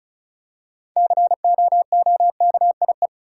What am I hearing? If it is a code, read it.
Morse code, heard as a keyed tone, about 36 wpm, 700 Hz. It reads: COOKIE
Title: Six Letter Words - 35wpm